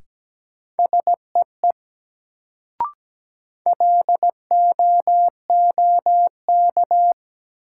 Morse code, heard as a keyed tone, about 17 wpm, 700 Hz. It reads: SEE LOOK
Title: Sentences from Top 100 Words - New Word Drill - 17wpm